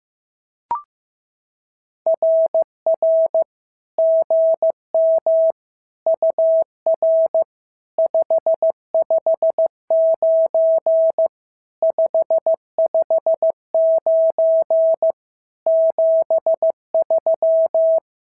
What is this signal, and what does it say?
Morse code, heard as a keyed tone, about 14 wpm, 650 Hz. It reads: RR GM UR 559 559 73
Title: SOTA QSO Practice 15wpm